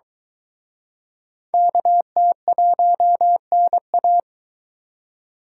Morse code, heard as a keyed tone, about 23 wpm, 700 Hz. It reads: KT1NA